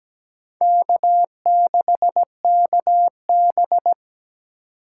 Morse code, heard as a keyed tone, about 17 wpm, 700 Hz. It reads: K6KB
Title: USA Call Signs - 4 Characters - 2x Character Spacing - 17wpm